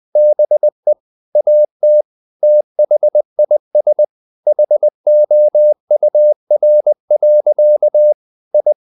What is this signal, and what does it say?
Morse code, heard as a keyed tone, about 20 wpm, 600 Hz. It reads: BE AT THIS HOUR. I
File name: the old man and the sea / the old man and the sea part 1 20wpm